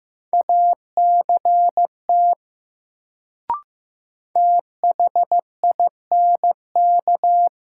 Morse code, heard as a keyed tone, about 15 wpm, 700 Hz. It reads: ACT THINK
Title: Sentences from Top 200 Words - New Word Drill - 15wpm